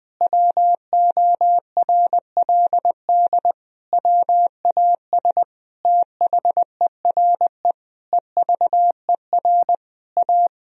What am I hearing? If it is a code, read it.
Morse code, heard as a keyed tone, about 20 wpm, 700 Hz. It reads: WORLD WAS THERE EVER A
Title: Sentences from Top 1000 Words - Review Part 4 - 20wpm